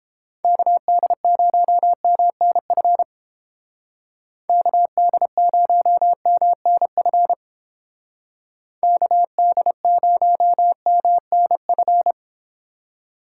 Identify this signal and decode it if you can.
Morse code, heard as a keyed tone, about 29 wpm, 700 Hz. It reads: KD0MNF KD0MNF KD0MNF